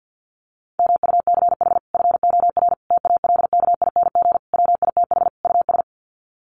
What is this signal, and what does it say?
Morse code, heard as a keyed tone, about 50 wpm, 700 Hz. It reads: MUCH FOR TALKING WITH US
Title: Sets of 5 Words Encore 50wpm